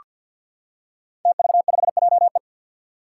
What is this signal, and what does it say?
Morse code, heard as a keyed tone, about 47 wpm, 700 Hz. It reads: TV5JE
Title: Five Challenging Letter-Number Combinations 50wpm